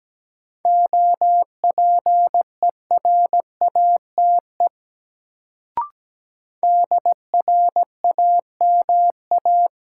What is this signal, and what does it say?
Morse code, heard as a keyed tone, about 17 wpm, 700 Hz. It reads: OPERATE DRAMA
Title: Sentences from Top 1500 Words - New Word Drill - 17wpm